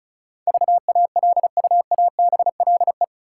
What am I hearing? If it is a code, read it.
Morse code, heard as a keyed tone, about 34 wpm, 700 Hz. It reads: VALUABLE
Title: Sentences from Top 1100 Words - New Word Drill Alt - 35wpm